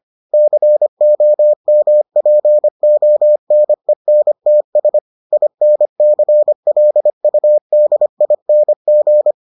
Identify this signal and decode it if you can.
Morse code, heard as a keyed tone, about 25 wpm, 600 Hz. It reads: COMPONENTS INCLUDING